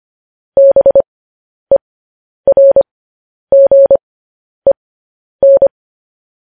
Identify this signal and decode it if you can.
Morse code, heard as a keyed tone, about 25 wpm, spaced as for 10 wpm, 550 Hz. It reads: BERGEN
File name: podcast / News headlines-1x-25x10